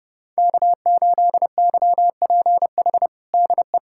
Morse code, heard as a keyed tone, about 30 wpm, 700 Hz. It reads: K8YPH DE
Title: USA Callsign DE Callsign - 5 Characters - 30wpm